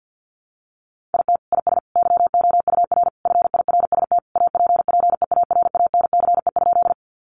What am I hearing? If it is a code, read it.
Morse code, heard as a keyed tone, about 50 wpm, 700 Hz. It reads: IT IS YOUR FIRST APPEARANCE?
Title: Sentences from Top 1100 Words 50wpm